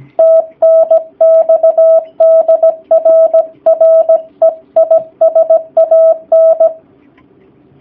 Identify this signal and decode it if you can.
Morse code, heard as a keyed tone, about 17 wpm, 650 Hz, with radio noise. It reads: TNX DR REI SAN
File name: jh8cbh / 2019cw / sp16- w